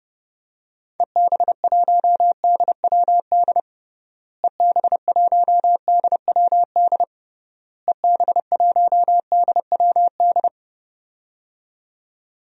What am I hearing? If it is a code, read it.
Morse code, heard as a keyed tone, about 30 wpm, 700 Hz. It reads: EB1DWD EB1DWD EB1DWD
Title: DXCC-100 Call Signs - 6 Characters - Repeated 3x - 30wpm